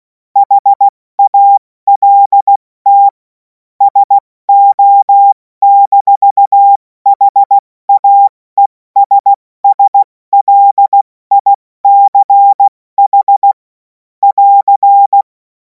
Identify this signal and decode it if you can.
Morse code, heard as a keyed tone, about 16 wpm, 800 Hz. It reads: HALT SO-HAESSLICH <AR>
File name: Morsekurs / Witze / Witz 02. Juni 2026 70 F80